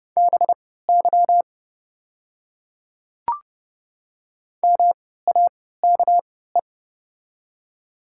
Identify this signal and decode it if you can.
Morse code, heard as a keyed tone, about 30 wpm, spaced as for 17 wpm, 700 Hz. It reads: BY MAKE
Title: 3x Character Spacing - Top 100 Words - 30wpm